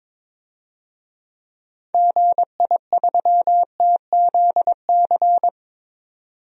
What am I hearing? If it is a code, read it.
Morse code, heard as a keyed tone, about 22 wpm, 700 Hz. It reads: GI3TZC
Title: DXCC-100 Call Signs - 6 Characters - 22wpm